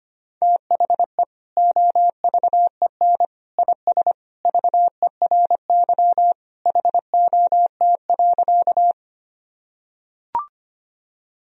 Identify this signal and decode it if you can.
Morse code, heard as a keyed tone, about 25 wpm, 700 Hz. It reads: THE OVEN IS VERY HOT.
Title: Sentences from Top 1000 Words - Review Part 4 - 25wpm